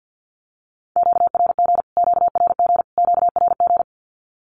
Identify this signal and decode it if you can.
Morse code, heard as a keyed tone, about 48 wpm, 700 Hz. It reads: QRG QRG QRG
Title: Q-Codes Repeated 3x 50wpm